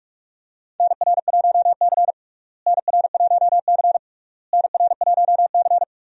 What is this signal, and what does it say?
Morse code, heard as a keyed tone, about 46 wpm, 700 Hz. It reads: NR1C NR1C NR1C
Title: USA Call Signs - 4 Characters - Repeated 3x - 45wpm